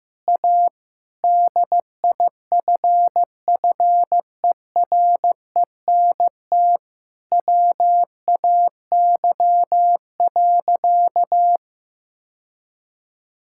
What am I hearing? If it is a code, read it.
Morse code, heard as a keyed tone, about 15 wpm, 700 Hz. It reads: A DIFFERENT WAY.
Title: Sentences from Top 1000 Words - Review Part 6 - 15wpm